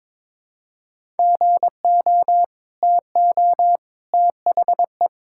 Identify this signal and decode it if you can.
Morse code, heard as a keyed tone, about 21 wpm, 700 Hz. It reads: GO TO THE
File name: practice / 22wpm / Sentences with 2-Word Phrases - 22wpm